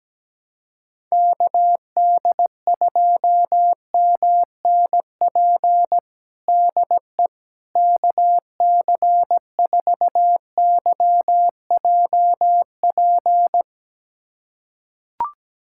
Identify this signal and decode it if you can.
Morse code, heard as a keyed tone, about 17 wpm, 700 Hz. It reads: KD2MNP DE KC4YJP
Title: USA Callsign DE Callsign 17wpm